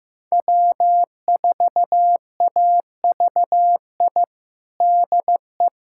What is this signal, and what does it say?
Morse code, heard as a keyed tone, about 15 wpm, 700 Hz. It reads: W4AVI DE
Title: USA Callsign DE Callsign - 5 Characters - 15wpm